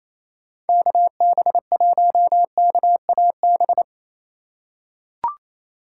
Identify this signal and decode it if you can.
Morse code, heard as a keyed tone, about 28 wpm, 700 Hz. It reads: KB1KAB
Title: USA Call Signs - 6 Characters - 4x Character Spacing - 28wpm